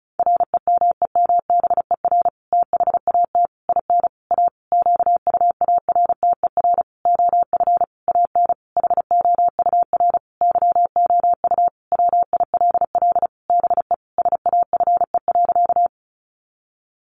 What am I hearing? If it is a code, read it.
Morse code, heard as a keyed tone, about 34 wpm, 700 Hz. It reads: REMEMBER THAT IN A QUARTER OF AN HOUR YOU WILL BE SAFE.
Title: Sentences from Top 1300 Words 35wpm